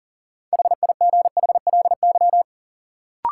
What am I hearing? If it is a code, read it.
Morse code, heard as a keyed tone, about 38 wpm, 700 Hz. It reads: HIGHLY
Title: Sentences from Top 1100 Words - New Word Drill Alt - 40wpm